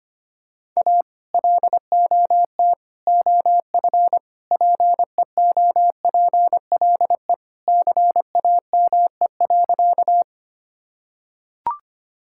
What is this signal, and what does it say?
Morse code, heard as a keyed tone, about 25 wpm, 700 Hz. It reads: A LOT OF PEOPLE CAME.
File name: practice / 25wpm / Sentences with 3-Word Phrases - 25wpm